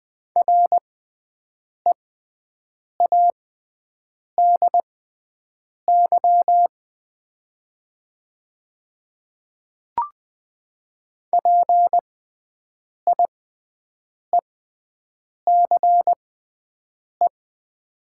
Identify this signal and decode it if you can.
Morse code, heard as a keyed tone, about 20 wpm, spaced as for 7 wpm, 700 Hz. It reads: READY PIECE
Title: 6x Character Spacing - Top 500 Words - 20wpm